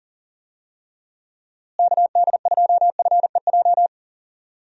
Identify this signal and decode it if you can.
Morse code, heard as a keyed tone, about 40 wpm, 700 Hz. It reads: KD2FEJ
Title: USA Call Signs - 6 Characters - Repeated 2x - 40wpm